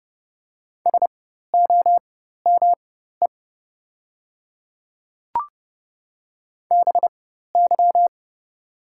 Morse code, heard as a keyed tone, about 30 wpm, spaced as for 14 wpm, 700 Hz. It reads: SOME BY
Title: 4x Character Spacing - Top 100 Words - 30wpm